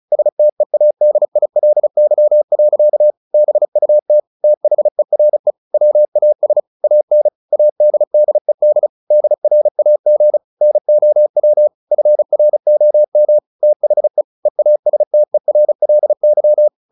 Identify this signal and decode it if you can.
Morse code, heard as a keyed tone, about 34 wpm, 600 Hz. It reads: STEADILY. BUT THERE WAS AN ADDED DRAG NOW FROM THE EASTERLY
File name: the old man and the sea / the old man and the sea 1 35wpm